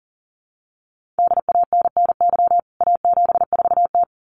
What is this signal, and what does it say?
Morse code, heard as a keyed tone, about 40 wpm, 700 Hz. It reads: DANNY A74T
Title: CWT Exchange 40wpm